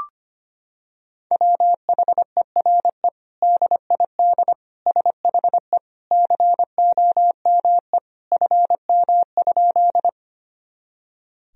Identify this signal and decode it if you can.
Morse code, heard as a keyed tone, about 25 wpm, 700 Hz. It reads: WHERE DID SHE COME FM?
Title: Sentences from Top 200 Words and Abbreviations 25wpm